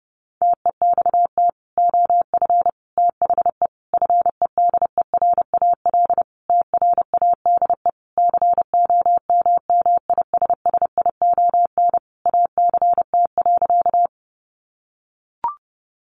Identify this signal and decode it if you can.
Morse code, heard as a keyed tone, about 30 wpm, 700 Hz. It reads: TEXT OF THE FEDERAL TRADE COMMISSION ACT.
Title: Sentences from Top 1400 Words 30wpm